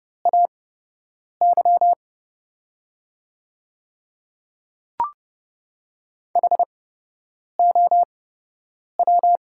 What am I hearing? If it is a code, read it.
Morse code, heard as a keyed tone, about 30 wpm, spaced as for 8 wpm, 690 Hz. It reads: AY HOW
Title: 8x Character Spacing - Top 200 Words - 30wpm